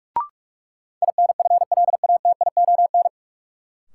Morse code, heard as a keyed tone, about 45 wpm, 700 Hz. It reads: INFLATION
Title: Sentences from Top 1100 Words - New Word Drill Alt - 45wpm